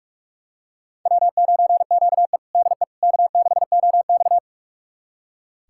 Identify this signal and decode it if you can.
Morse code, heard as a keyed tone, about 46 wpm, 700 Hz. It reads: W9QE DE K6OX
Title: USA Callsign DE Callsign - 4 Characters - 45wpm